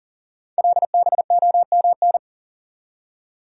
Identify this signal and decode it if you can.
Morse code, heard as a keyed tone, about 38 wpm, 700 Hz. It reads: LBOMN
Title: Code Groups - 5 Letters - 40wpm